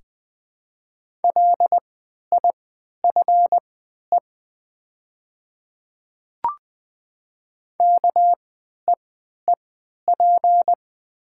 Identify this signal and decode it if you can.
Morse code, heard as a keyed tone, about 20 wpm, spaced as for 11 wpm, 700 Hz. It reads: LIFE KEEP
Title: 3x Character Spacing - Top 300 Words - 20wpm